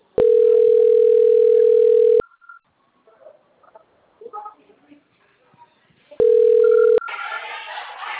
Ringback tone (North America), 440 + 480 Hz.